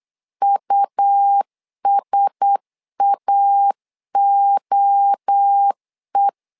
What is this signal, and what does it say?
Morse code, heard as a keyed tone, about 8 wpm, 800 Hz. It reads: USAOE
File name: LEZIONI / 4 Q1S3 8wpm 3037